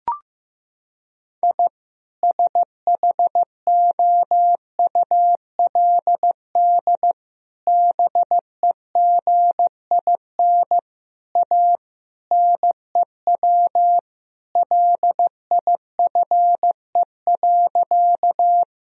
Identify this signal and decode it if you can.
Morse code, heard as a keyed tone, about 16 wpm, 700 Hz. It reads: I SHOULD BEGIN A NEW LIFE.
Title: Sentences from Top 1000 Words - Review Part 1 - 15wpm